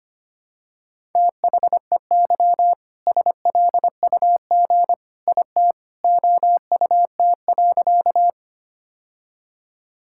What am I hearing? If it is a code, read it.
Morse code, heard as a keyed tone, about 25 wpm, 700 Hz. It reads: THEY SLUG IT OUT.
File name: practice / 25wpm / All-Age Appropriate Jokes - 25wpm